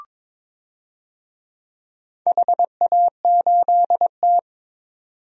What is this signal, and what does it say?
Morse code, heard as a keyed tone, about 22 wpm, 700 Hz. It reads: HA8T
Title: DXCC-100 Call Signs - 4 Characters - 22wpm